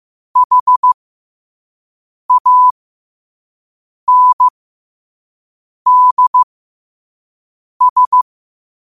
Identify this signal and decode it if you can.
Morse code, heard as a keyed tone, about 15 wpm, spaced as for 5 wpm, 1000 Hz. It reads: HANDS
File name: qotdfiles / 5wpm